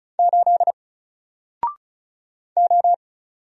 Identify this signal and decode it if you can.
Morse code, heard as a keyed tone, about 36 wpm, 700 Hz. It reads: 8 O
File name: Single-Letter-Number-35wpm